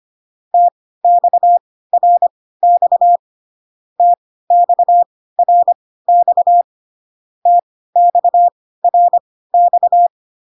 Morse code, heard as a keyed tone, about 25 wpm, spaced as for 16 wpm, 700 Hz. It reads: TXRX TXRX TXRX